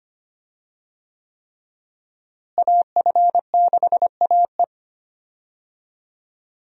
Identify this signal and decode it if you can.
Morse code, heard as a keyed tone, about 25 wpm, 700 Hz. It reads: AF6AE
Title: USA Call Signs - 5 Characters - Repeated 3x - 25wpm